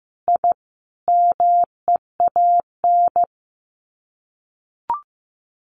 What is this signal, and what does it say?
Morse code, heard as a keyed tone, about 15 wpm, 700 Hz. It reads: I MEAN
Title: Sets of 5 Words 15wpm